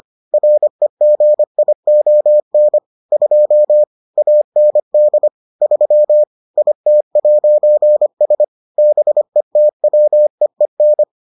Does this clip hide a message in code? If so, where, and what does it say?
Yes: Morse code from 0.3 to 11.0 s, heard as a keyed tone, about 25 wpm, 600 Hz. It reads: REGION 2 AND 3 IT'S BETWEEN